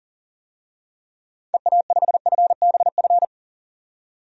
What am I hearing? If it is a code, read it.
Morse code, heard as a keyed tone, about 40 wpm, 700 Hz. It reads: EA5FBF
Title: DXCC-100 Call Signs - 6 Characters - 40wpm